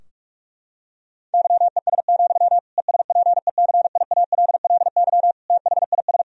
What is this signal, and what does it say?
Morse code, heard as a keyed tone, about 46 wpm, 700 Hz. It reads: YES, ESPECIALLY THIS
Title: Sentences from Top 1000 Words - Review Part 4 - 45wpm